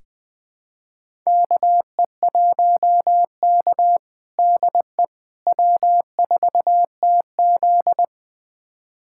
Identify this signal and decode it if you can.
Morse code, heard as a keyed tone, about 20 wpm, 700 Hz. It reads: KE1K DE W4TZ
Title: USA Callsign DE Callsign - 4 Characters - 20wpm